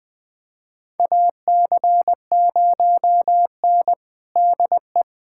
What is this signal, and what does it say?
Morse code, heard as a keyed tone, about 20 wpm, 700 Hz. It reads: AC0N DE
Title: USA Callsign DE Callsign 20wpm